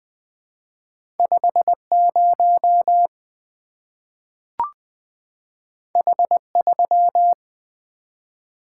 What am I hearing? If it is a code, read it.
Morse code, heard as a keyed tone, about 20 wpm, 700 Hz. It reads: 50 H3
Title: Two Letter-Number Combinations - 20wpm